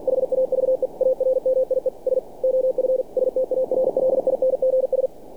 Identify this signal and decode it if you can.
Morse code, heard as a keyed tone, about 48 wpm, 520 Hz. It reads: HAVE ARMIES OF STARLINGS
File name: archived / CRSnet